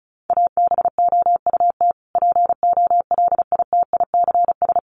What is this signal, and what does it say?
Morse code, heard as a keyed tone, about 34 wpm, 700 Hz. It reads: ABOUT POLITICS
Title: Sentences from Top 1000 Words 35wpm